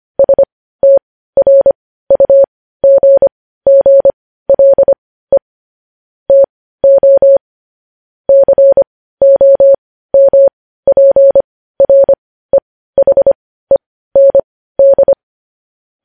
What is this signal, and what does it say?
Morse code, heard as a keyed tone, about 25 wpm, spaced as for 15 wpm, 550 Hz. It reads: STRUGGLE TO COMPREHEND
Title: News Headlines at 15